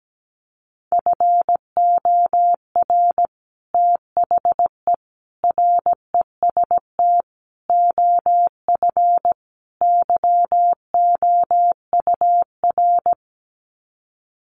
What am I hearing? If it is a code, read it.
Morse code, heard as a keyed tone, about 17 wpm, 700 Hz. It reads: FOR THE REST OF YOUR
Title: Sets of 5 Words Encore 17wpm